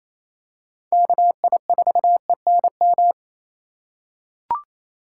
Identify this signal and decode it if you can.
Morse code, heard as a keyed tone, about 28 wpm, 700 Hz. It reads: KI4ENM
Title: USA Call Signs - 6 Characters - 3x Character Spacing - 28wpm